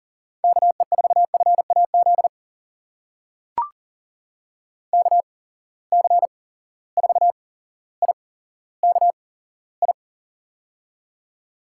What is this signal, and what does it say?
Morse code, heard as a keyed tone, about 39 wpm, spaced as for 17 wpm, 700 Hz. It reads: KE4FAZ KC4IKI
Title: USA Call Signs - 8x Character Spacing - 40wpm